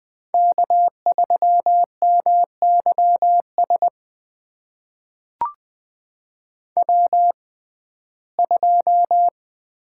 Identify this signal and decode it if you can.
Morse code, heard as a keyed tone, about 20 wpm, spaced as for 15 wpm, 700 Hz. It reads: K3MYS W2
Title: USA Call Signs - 5 Characters - 6x Character Spacing - 20wpm